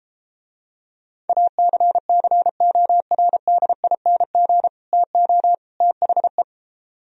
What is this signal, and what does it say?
Morse code, heard as a keyed tone, about 33 wpm, 700 Hz. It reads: ACCORDING TO THE